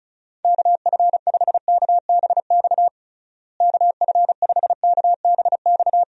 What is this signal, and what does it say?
Morse code, heard as a keyed tone, about 36 wpm, 700 Hz. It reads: KF5KBX KF5KBX
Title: USA Call Signs - 6 Characters - Repeated 2x - 35wpm